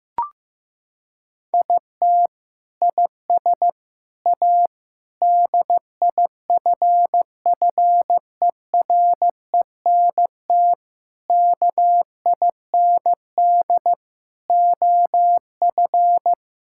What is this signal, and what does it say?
Morse code, heard as a keyed tone, about 16 wpm, 700 Hz. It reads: IT IS A DIFFERENT KIND OF
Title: Sentences from Top 1000 Words - Review Part 5 - 15wpm